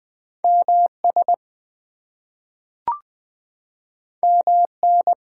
Morse code, heard as a keyed tone, about 20 wpm, 700 Hz. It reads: MS MN
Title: US States and Canadian Provinces 20wpm